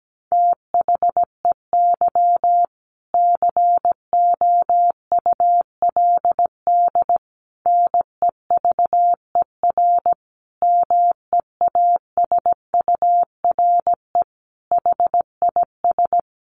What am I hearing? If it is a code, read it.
Morse code, heard as a keyed tone, about 17 wpm, 700 Hz. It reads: THEY COULD NEVER MEASURE HIS
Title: Sentences from Top 1200 Words 17wpm